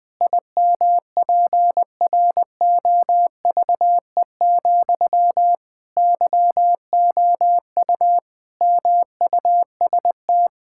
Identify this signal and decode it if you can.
Morse code, heard as a keyed tone, about 20 wpm, 700 Hz. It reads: IMPROVE, YOU MUST
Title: Sentences from Top 1000 Words - Review Part 5 - 20wpm